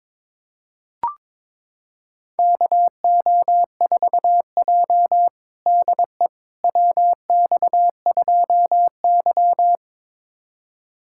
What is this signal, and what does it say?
Morse code, heard as a keyed tone, about 22 wpm, 700 Hz. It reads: KO4J DE WX2Y
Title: USA Callsign DE Callsign - 4 Characters - 22wpm